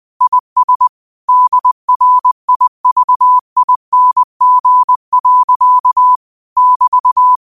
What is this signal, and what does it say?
Morse code, heard as a keyed tone, about 20 wpm, 1000 Hz. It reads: IS DRIVING. <BT>
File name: qotdfiles / 20wpm